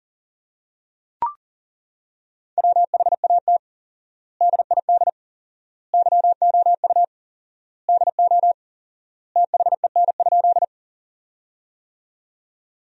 Morse code, heard as a keyed tone, about 40 wpm, 700 Hz. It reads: WHAT DID YOU DO THEN?
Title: 4x Word-Spacing - Sentences from Top 100 Words - 40wpm